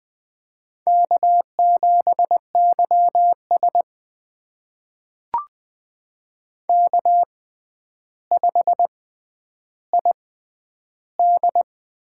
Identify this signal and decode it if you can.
Morse code, heard as a keyed tone, about 20 wpm, spaced as for 10 wpm, 700 Hz. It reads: K7YS K5ID
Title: USA Call Signs - 4 Characters - 6x Character Spacing - 20wpm